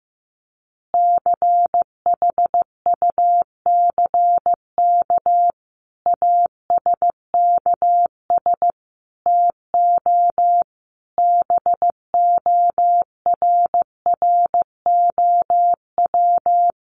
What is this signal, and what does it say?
Morse code, heard as a keyed tone, about 15 wpm, 700 Hz. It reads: CHUCK ASKS TO BORROW
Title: Additional Easy Phrases 15wpm